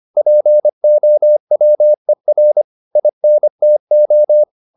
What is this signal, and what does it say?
Morse code, heard as a keyed tone, about 25 wpm, 600 Hz. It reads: POWER INTO
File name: SS305 25